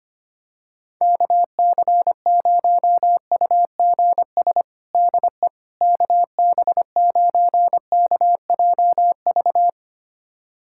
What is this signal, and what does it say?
Morse code, heard as a keyed tone, about 25 wpm, 700 Hz. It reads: KC0UGS DE KB9KJV
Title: USA Callsign DE Callsign - 6 Characters - 25wpm